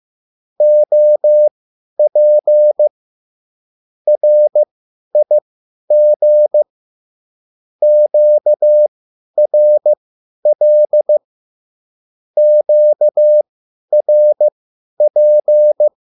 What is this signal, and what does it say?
Morse code, heard as a keyed tone, about 15 wpm, spaced as for 10 wpm, 600 Hz. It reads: OP RIG QRL QRP
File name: QSL / CW lesson